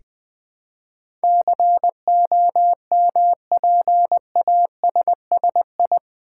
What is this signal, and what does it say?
Morse code, heard as a keyed tone, about 20 wpm, 700 Hz. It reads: COMPASSI
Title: Build A Word - 4 Letters At A Time - 10-Letter Words - 20wpm